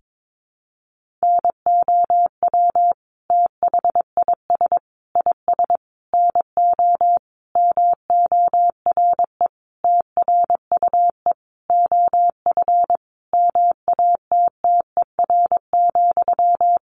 Morse code, heard as a keyed tone, about 22 wpm, 700 Hz. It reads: NOW THIS IS NO MORE TRUE OF MATTER,
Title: Sentences from Top 1400 Words 22wpm